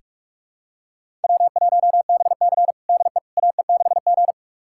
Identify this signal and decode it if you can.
Morse code, heard as a keyed tone, about 46 wpm, 700 Hz. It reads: W1BC DE AE6G
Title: USA Callsign DE Callsign - 4 Characters - 45wpm